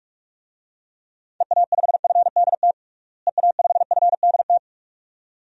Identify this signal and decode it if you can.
Morse code, heard as a keyed tone, about 43 wpm, 700 Hz. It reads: EA5FDT EA5FDT
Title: DXCC-100 Call Signs - 6 Characters - Repeated 2x - 45wpm